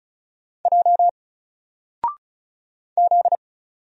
Morse code, heard as a keyed tone, about 34 wpm, 700 Hz. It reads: J Z
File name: Single-Letter-Number-35wpm